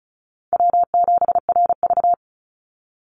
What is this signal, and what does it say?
Morse code, heard as a keyed tone, about 34 wpm, 700 Hz. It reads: W7RV
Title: NA SOTA Chaser Callsigns 35wpm